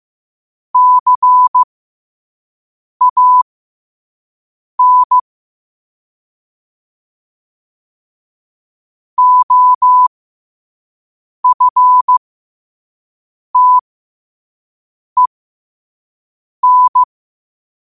Morse code, heard as a keyed tone, about 15 wpm, spaced as for 5 wpm, 1000 Hz. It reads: CAN OFTEN